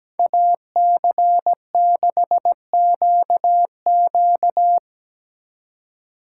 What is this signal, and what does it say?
Morse code, heard as a keyed tone, about 20 wpm, 700 Hz. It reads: AC6QQ
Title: Speed Racing - USA Call Signs 17wpm